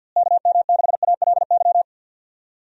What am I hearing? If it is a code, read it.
Morse code, heard as a keyed tone, about 47 wpm, 700 Hz. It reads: KM6ALY
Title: USA Call Signs - 6 Characters - Repeated 2x - 50wpm